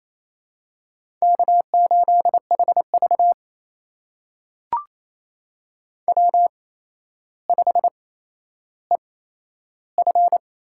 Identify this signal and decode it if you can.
Morse code, heard as a keyed tone, about 27 wpm, spaced as for 12 wpm, 700 Hz. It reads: K8HV W5EF
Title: USA Call Signs - 4 Characters - 8x Character Spacing - 28wpm